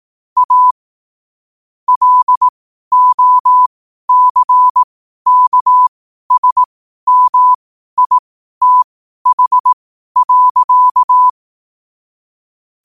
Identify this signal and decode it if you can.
Morse code, heard as a keyed tone, about 18 wpm, spaced as for 12 wpm, 1000 Hz. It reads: A LOCKSMITH.
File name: qotdfiles / 12wpm